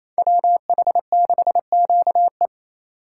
Morse code, heard as a keyed tone, about 32 wpm, 700 Hz. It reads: WH6QE
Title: Speed Racing - USA 2x2 Call Signs 28wpm